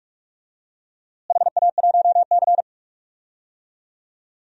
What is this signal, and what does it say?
Morse code, heard as a keyed tone, about 41 wpm, 700 Hz. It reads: HA1C
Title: DXCC-100 Call Signs - 4 Characters - Repeated 3x - 45wpm